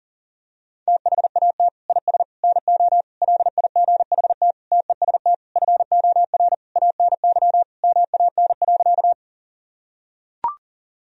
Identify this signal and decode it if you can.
Morse code, heard as a keyed tone, about 40 wpm, 700 Hz. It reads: THAT IS NO LIGHT TEST FOR ANY MAN.
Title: Sentences from Top 1000 Words - Review Part 3 - 40wpm